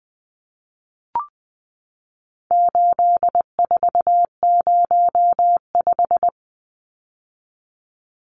Morse code, heard as a keyed tone, about 20 wpm, 700 Hz. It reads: 8405
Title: Numbers - 4-Digit 20wpm